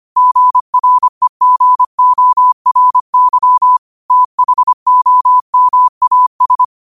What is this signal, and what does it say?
Morse code, heard as a keyed tone, about 25 wpm, 1000 Hz. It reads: GREGORY THOMAS